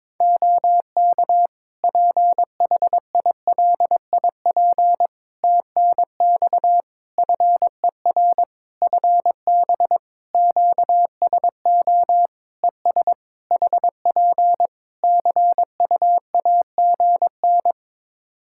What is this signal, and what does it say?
Morse code, heard as a keyed tone, about 22 wpm, 700 Hz. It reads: OK PHILIP TNX FER FB QSO ES HP CUAGN
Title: K1USN SST QSO Practice 22wpm